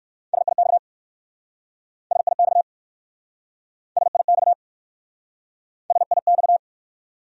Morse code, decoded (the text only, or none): SIX SIX SIX SIX